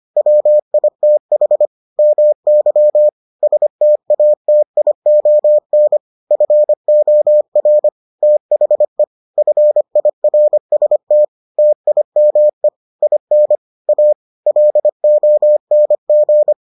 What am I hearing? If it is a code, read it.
Morse code, heard as a keyed tone, about 25 wpm, 600 Hz. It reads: WITH MY STATION FOR THE FIRST TIME IN A LONG